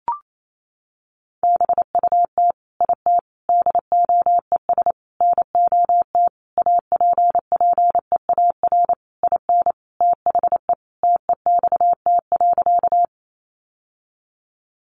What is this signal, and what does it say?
Morse code, heard as a keyed tone, about 28 wpm, 700 Hz. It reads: BUT IT DOES NOT APPEAR IN THE TEXT.
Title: Sentences from Top 1200 Words 28wpm